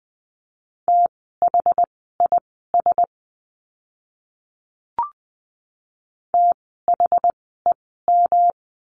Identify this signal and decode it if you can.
Morse code, heard as a keyed tone, about 20 wpm, spaced as for 14 wpm, 700 Hz. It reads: THIS THEM
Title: 185 Lesson 30f - Letter b - Words 20wpm